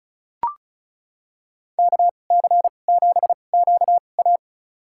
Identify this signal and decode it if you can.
Morse code, heard as a keyed tone, about 36 wpm, spaced as for 25 wpm, 700 Hz. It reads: KC7QA
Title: USA Call Signs - 5 Characters - 2x Character Spacing - 35wpm